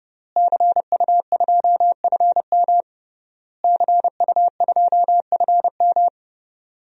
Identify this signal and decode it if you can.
Morse code, heard as a keyed tone, about 32 wpm, 700 Hz. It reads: CU2FM CU2FM
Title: DXCC-100 Call Signs - 5 Characters - Repeated 3x - 30wpm